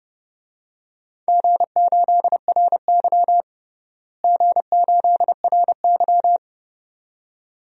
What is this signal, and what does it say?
Morse code, heard as a keyed tone, about 32 wpm, 700 Hz. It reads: G8RY G8RY
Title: DXCC-100 Call Signs - 4 Characters - Repeated 2x - 30wpm